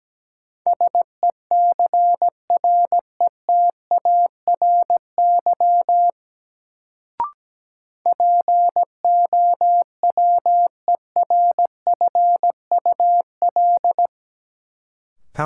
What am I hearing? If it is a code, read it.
Morse code, heard as a keyed tone, about 17 wpm, 700 Hz. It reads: SECRETARY POWERFUL
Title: Sentences from Top 1300 Words - New Word Drill Alt - 17wpm